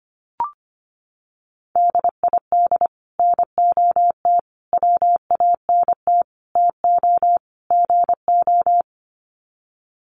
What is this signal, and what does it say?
Morse code, heard as a keyed tone, about 25 wpm, 700 Hz. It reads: DID NOT WANT TO GO
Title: Sets of 5 Words 25wpm